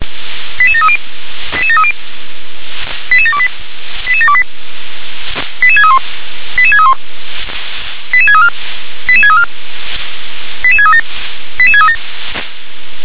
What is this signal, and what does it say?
Five-tone selective call (ZVEI), heard as quick runs of five tones, tones of about 70 ms. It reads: 88730 88730 88728 88728 88631 88631 88634 88634 88637 88637